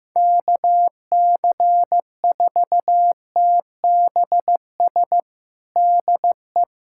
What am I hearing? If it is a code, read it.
Morse code, heard as a keyed tone, about 15 wpm, 700 Hz. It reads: KC4TBS DE
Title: USA Callsign DE Callsign - 6 Characters - 15wpm